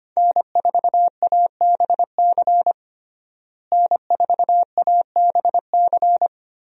Morse code, heard as a keyed tone, about 25 wpm, 700 Hz. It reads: N4ABC N4ABC
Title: USA Call Signs - 5 Characters - Repeated 3x - 25wpm